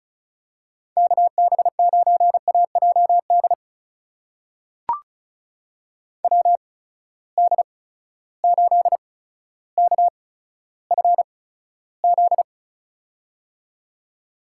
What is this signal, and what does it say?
Morse code, heard as a keyed tone, about 36 wpm, spaced as for 15 wpm, 700 Hz. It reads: KB9AJD WD8KFZ
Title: USA Call Signs - 6 Characters - 8x Character Spacing - 35wpm